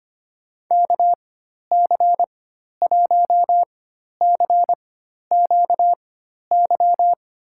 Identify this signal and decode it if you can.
Morse code, heard as a keyed tone, about 25 wpm, spaced as for 12 wpm, 700 Hz. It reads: KC1CQY
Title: USA Call Signs - 6 Characters - 4x Character Spacing - 25wpm